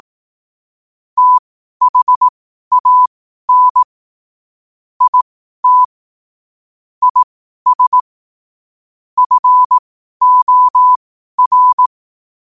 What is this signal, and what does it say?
Morse code, heard as a keyed tone, about 18 wpm, spaced as for 12 wpm, 1000 Hz. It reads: THAN IT IS FOR
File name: qotdfiles / 12wpm